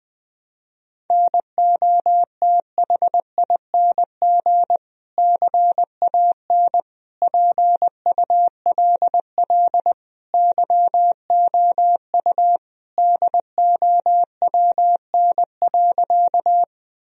Morse code, heard as a keyed tone, about 19 wpm, 700 Hz. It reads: NOTHING CAN PULL YOU DOWN.
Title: Sentences from Top 1000 Words - Review Part 2 - 20wpm